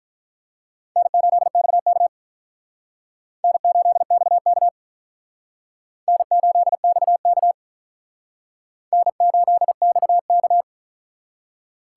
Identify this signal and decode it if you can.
Morse code, heard as a keyed tone, about 43 wpm, 700 Hz. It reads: N8XK N8XK N8XK N8XK
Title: Speed Racing - USA 1x2 Call Signs 35wpm